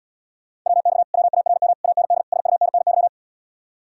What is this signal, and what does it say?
Morse code, heard as a keyed tone, about 80 wpm, 700 Hz. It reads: WX WARM AND SUNNY